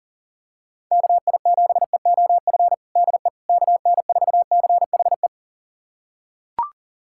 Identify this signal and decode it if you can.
Morse code, heard as a keyed tone, about 40 wpm, 700 Hz. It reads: KI7EOF DE KN4CHE
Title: USA Callsign DE Callsign - 6 Characters - 40wpm